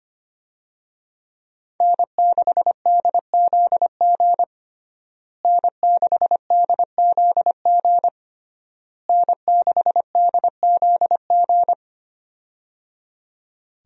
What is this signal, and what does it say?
Morse code, heard as a keyed tone, about 25 wpm, 700 Hz. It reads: N6DZG N6DZG N6DZG
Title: USA Call Signs - 5 Characters - Repeated 3x - 25wpm